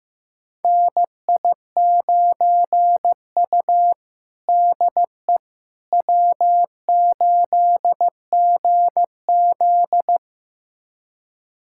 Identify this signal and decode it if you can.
Morse code, heard as a keyed tone, about 16 wpm, 700 Hz. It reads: NI9U DE W8GZ
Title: USA Callsign DE Callsign - 4 Characters - 15wpm